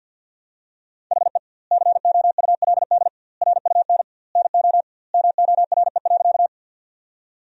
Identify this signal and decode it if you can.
Morse code, heard as a keyed tone, about 51 wpm, 700 Hz. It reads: HE COULD RUN NO MORE.
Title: 2x Word-Spacing - Sentences from Top 300 Words - 50wpm